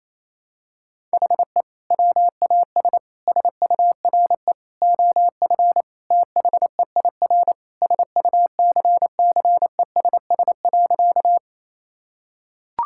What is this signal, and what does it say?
Morse code, heard as a keyed tone, about 28 wpm, 700 Hz. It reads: HE WAS SURE OF THEIR SUCCESS.
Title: Sentences from Top 1000 Words - Review Part 6 - 28wpm